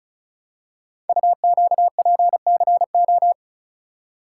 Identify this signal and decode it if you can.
Morse code, heard as a keyed tone, about 36 wpm, 690 Hz. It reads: UQPCO
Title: Code Groups - 5 Letters - 35wpm